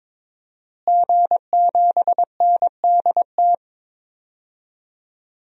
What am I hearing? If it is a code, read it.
Morse code, heard as a keyed tone, about 22 wpm, 700 Hz. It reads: G7NDT
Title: DXCC-100 Call Signs - 5 Characters - Repeated 2x - 22wpm